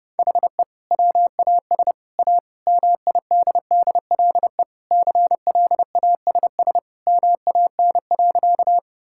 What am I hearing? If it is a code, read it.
Morse code, heard as a keyed tone, about 30 wpm, 700 Hz. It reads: HE WAS A MIDDLE CLASS MAN.
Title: Sentences from Top 1000 Words - Review Part 6 - 30wpm